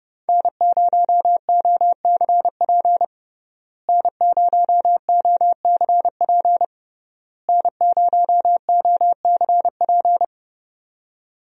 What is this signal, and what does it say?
Morse code, heard as a keyed tone, about 30 wpm, 700 Hz. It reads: N0OCP N0OCP N0OCP
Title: USA Call Signs - 5 Characters - Repeated 3x - 30wpm